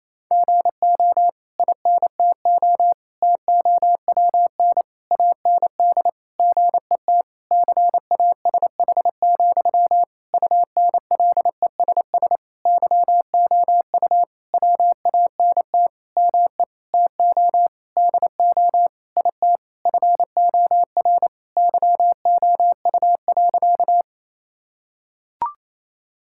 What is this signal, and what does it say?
Morse code, heard as a keyed tone, about 28 wpm, 700 Hz. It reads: GO INTO TOWN AND GET CASH, UNLESS YOU WANT ME TO DO IT FOR YOU.
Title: Sentences from Top 1000 Words - Review Part 6 - 28wpm